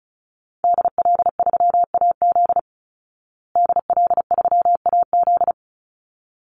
Morse code, heard as a keyed tone, about 34 wpm, 700 Hz. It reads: DL3AZ DL3AZ
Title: Call Signs Repeated 3x - 35wpm